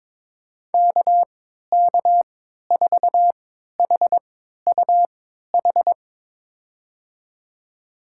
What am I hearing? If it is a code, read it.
Morse code, heard as a keyed tone, about 22 wpm, spaced as for 13 wpm, 700 Hz. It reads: KK4HUH
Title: USA Call Signs - 6 Characters - 3x Character Spacing - 22wpm